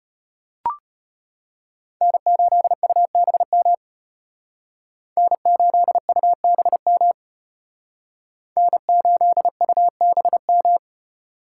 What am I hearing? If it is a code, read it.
Morse code, heard as a keyed tone, about 32 wpm, 700 Hz. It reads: N8UBM N8UBM N8UBM